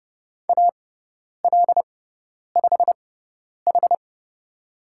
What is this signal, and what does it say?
Morse code, heard as a keyed tone, about 30 wpm, spaced as for 10 wpm, 700 Hz. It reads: AL5H
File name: Lesson 16g - Letter c - Call Signs 30wpm